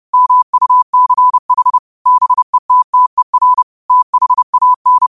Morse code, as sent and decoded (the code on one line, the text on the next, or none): -- ..- -.-. .... / -... - - . .-. / - .... .- -.
MUCH BETTER THAN